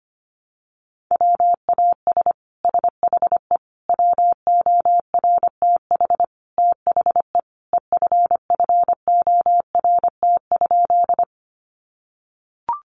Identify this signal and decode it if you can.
Morse code, heard as a keyed tone, about 25 wpm, 700 Hz. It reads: WAS SHE WORTH THE EFFORT?
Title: Sentences from Top 1200 Words 25wpm